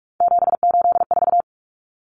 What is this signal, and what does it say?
Morse code, heard as a keyed tone, about 44 wpm, 700 Hz. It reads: 784
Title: Numbers - 3-Digit 45wpm